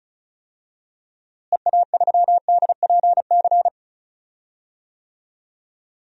Morse code, heard as a keyed tone, about 34 wpm, 700 Hz. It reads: EA3DPC
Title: DXCC-100 Call Signs - 6 Characters - Repeated 3x - 35wpm